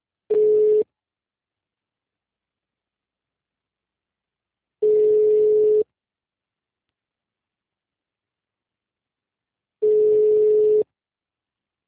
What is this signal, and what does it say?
Ringback tone (Europe, 425 Hz).
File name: out- -2008- - - .22